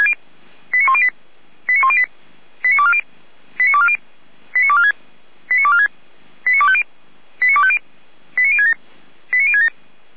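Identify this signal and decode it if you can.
Five-tone selective call (ZVEI), heard as quick runs of five tones, tones of about 70 ms. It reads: 89108 89108 89240 89240 89246 89246 89250 89250 89067 89067
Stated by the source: ZVEI five-tone selective call 89337, received by SDR for Feuerwehr Orlamünde.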